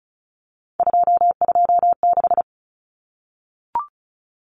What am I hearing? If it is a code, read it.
Morse code, heard as a keyed tone, about 34 wpm, 700 Hz. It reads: 226
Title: Numbers - 3-Digit 35wpm